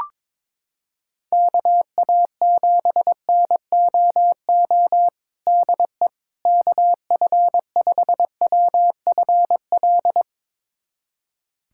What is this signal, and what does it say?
Morse code, heard as a keyed tone, about 22 wpm, 700 Hz. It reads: KA7NOO DE KF5WFL
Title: USA Callsign DE Callsign 22wpm